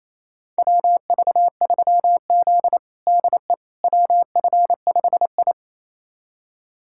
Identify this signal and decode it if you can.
Morse code, heard as a keyed tone, about 28 wpm, 700 Hz. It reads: WV3Z DE WF5I
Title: USA Callsign DE Callsign - 4 Characters - 28wpm